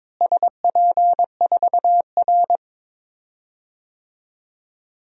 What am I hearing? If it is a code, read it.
Morse code, heard as a keyed tone, about 22 wpm, 700 Hz. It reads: SP4R
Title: DXCC-100 Call Signs - 4 Characters - Repeated 2x - 22wpm